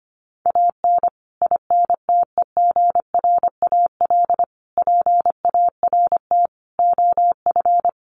Morse code, heard as a keyed tone, about 25 wpm, 700 Hz. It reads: AN INTEGRAL PART OF
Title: Sets of 5 Words 25wpm